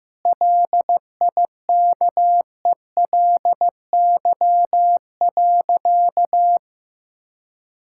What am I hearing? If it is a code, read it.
Morse code, heard as a keyed tone, about 15 wpm, 700 Hz. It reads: LIKELY.
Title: Sentences from Top 1600 Words 15wpm